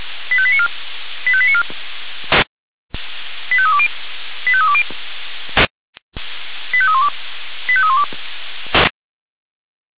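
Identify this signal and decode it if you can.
Five-tone selective call (ZVEI), heard as quick runs of five tones, tones of about 70 ms. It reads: 85894 85894 85320 85320 85312 85312
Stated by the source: Five-tone ZVEI selective call received on an SDR.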